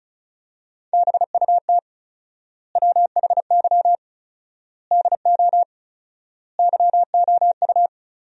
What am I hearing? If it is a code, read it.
Morse code, heard as a keyed tone, about 34 wpm, 700 Hz. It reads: BUT WHY DO YOU
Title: 4x Word-Spacing - Sentences from Top 200 Words - 35wpm